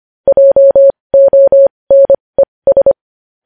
Morse code, heard as a keyed tone, about 25 wpm, spaced as for 20 wpm, 550 Hz. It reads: JONES
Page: news-headlines-in-morse-code-at-20-wpm